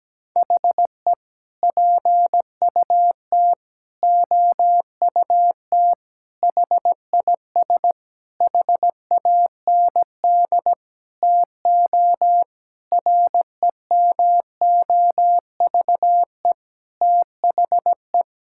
Morse code, heard as a keyed tone, about 17 wpm, 700 Hz. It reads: HE PUT OUT HIS HAND TO REMOVE THE
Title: Sentences from Top 1000 Words - Review Part 5 - 17wpm